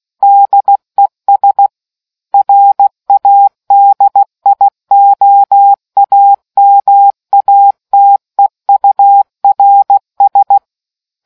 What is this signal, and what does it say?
Morse code, heard as a keyed tone, about 16 wpm, 800 Hz. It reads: DES RADIOAMATEURS